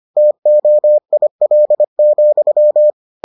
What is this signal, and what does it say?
Morse code, heard as a keyed tone, about 25 wpm, 600 Hz. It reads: TOIL,